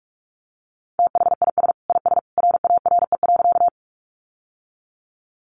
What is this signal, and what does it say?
Morse code, heard as a keyed tone, about 44 wpm, 700 Hz. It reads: THIS IS RARE.
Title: Sentences from Top 1100 Words 45wpm